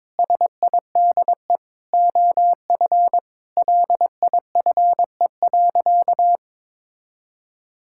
Morse code, heard as a keyed tone, about 22 wpm, 700 Hz. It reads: SIDE OF LIFE.
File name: practice / 22wpm / Sentences with 8-Word Phrases - 22wpm